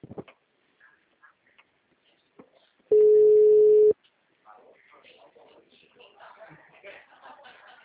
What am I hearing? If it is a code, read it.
Ringback tone (Europe, 425 Hz), with speech.
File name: out- -2017- - - .39888